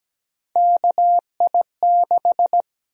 Morse code, heard as a keyed tone, about 17 wpm, 700 Hz. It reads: KI6
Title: USA Call Sign Prefixes - 17wpm